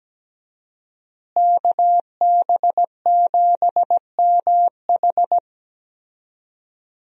Morse code, heard as a keyed tone, about 17 wpm, 700 Hz. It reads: KB7MH
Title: USA Call Signs - 5 Characters - Repeated 3x - 17wpm